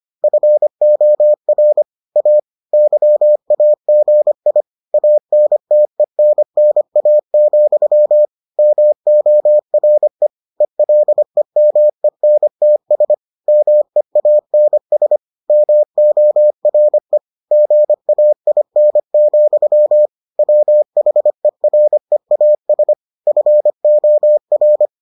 Morse code, heard as a keyed tone, about 25 wpm, 600 Hz. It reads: FOR A YAGI ANTENNA, MORE ELEMENTS MEANS MORE GAIN, WHEREAS FOR